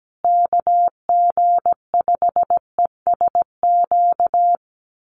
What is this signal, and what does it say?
Morse code, heard as a keyed tone, about 17 wpm, 700 Hz. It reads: KG5ESQ
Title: USA Call Signs Repeated 3x 17wpm